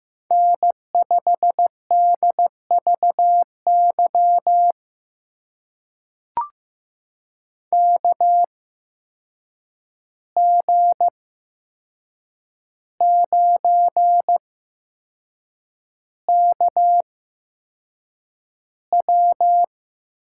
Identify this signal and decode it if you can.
Morse code, heard as a keyed tone, about 15 wpm, spaced as for 6 wpm, 700 Hz. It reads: N5DVY KG9KW
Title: USA Call Signs - 5 Characters - 8x Character Spacing - 15wpm